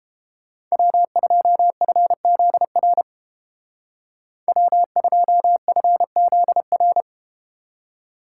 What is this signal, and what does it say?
Morse code, heard as a keyed tone, about 29 wpm, 700 Hz. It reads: W2FZR W2FZR